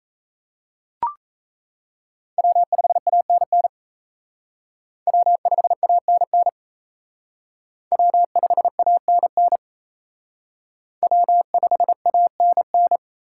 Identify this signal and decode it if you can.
Morse code, heard as a keyed tone, about 35 wpm, 700 Hz. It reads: W5ANN W5ANN W5ANN W5ANN
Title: Speed Racing - USA 1x3 Call Signs 28wpm